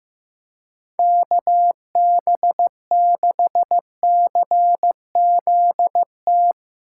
Morse code, heard as a keyed tone, about 15 wpm, 700 Hz. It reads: KB6CZT
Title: USA Call Signs - 6 Characters - 4x Character Spacing - 15wpm